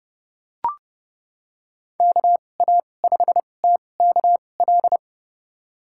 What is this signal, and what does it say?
Morse code, heard as a keyed tone, about 31 wpm, spaced as for 22 wpm, 700 Hz. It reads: KA5TKL
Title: USA Call Signs - 6 Characters - 2x Character Spacing - 30wpm